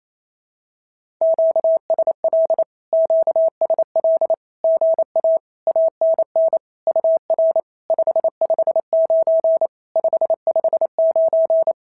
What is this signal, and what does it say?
Morse code, heard as a keyed tone, about 25 wpm, 650 Hz. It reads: QSL QSL GA ANN UR 559 559
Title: SOTA QSO Practice 28wpm